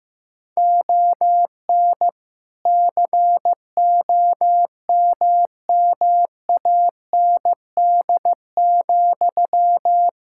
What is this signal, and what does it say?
Morse code, heard as a keyed tone, about 15 wpm, 700 Hz. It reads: ON COMMAND,
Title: Sentences from Top 1600 Words 15wpm